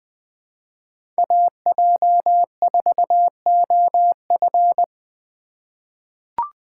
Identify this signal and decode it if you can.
Morse code, heard as a keyed tone, about 20 wpm, 700 Hz. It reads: AJ4OF
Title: USA Call Signs - 5 Characters - 4x Character Spacing - 20wpm